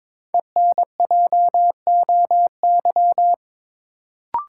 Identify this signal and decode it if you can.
Morse code, heard as a keyed tone, about 22 wpm, 700 Hz. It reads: ENJOY
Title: Sentences from Top 1000 Words - Review Part 4 - New Word Drill - 22wpm